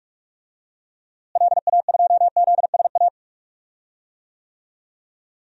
Morse code, heard as a keyed tone, about 46 wpm, 700 Hz. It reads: LA2ZSA
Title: DXCC-100 Call Signs - 6 Characters - Repeated 2x - 45wpm